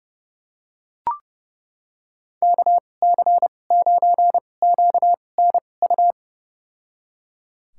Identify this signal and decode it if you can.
Morse code, heard as a keyed tone, about 30 wpm, spaced as for 22 wpm, 700 Hz. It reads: KC9QNU
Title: USA Call Signs - 6 Characters - 2x Character Spacing - 30wpm